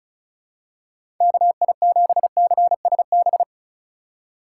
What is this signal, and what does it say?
Morse code, heard as a keyed tone, about 36 wpm, 700 Hz. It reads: KI7CSB
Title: USA Call Signs - 3x Character Spacing - 35wpm